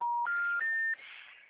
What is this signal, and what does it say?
Special information tone, with noise.
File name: out- -2038- - - .234